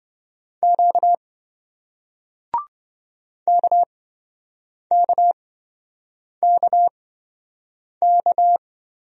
Morse code, heard as a keyed tone, about 28 wpm, spaced as for 7 wpm, 700 Hz. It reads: Q KKKK